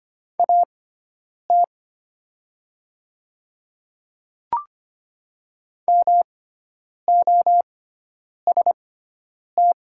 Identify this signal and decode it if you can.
Morse code, heard as a keyed tone, about 25 wpm, spaced as for 9 wpm, 700 Hz. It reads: AT MOST
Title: 6x Character Spacing - Top 300 Words - 25wpm